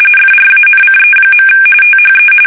RTTY, about 45.45 baud. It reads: FRANCE. THE
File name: fsk850